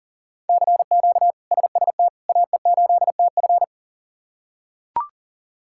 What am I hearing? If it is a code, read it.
Morse code, heard as a keyed tone, about 40 wpm, 700 Hz. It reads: CQ SST AE8TF
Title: K1USN SST QSO Practice - 40wpm